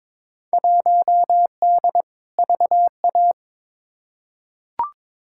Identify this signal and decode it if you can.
Morse code, heard as a keyed tone, about 22 wpm, 700 Hz. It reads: 1D VA
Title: Field Day Exchanges - 22wpm